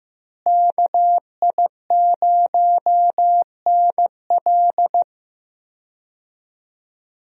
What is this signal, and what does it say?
Morse code, heard as a keyed tone, about 15 wpm, 700 Hz. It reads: KI0NL
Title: USA Call Signs - 5 Characters - Repeated 3x - 15wpm